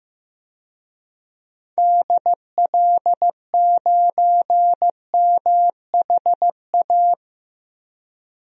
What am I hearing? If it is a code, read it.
Morse code, heard as a keyed tone, about 15 wpm, 700 Hz. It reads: DL9MHA
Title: DXCC-100 Call Signs - 6 Characters - 15wpm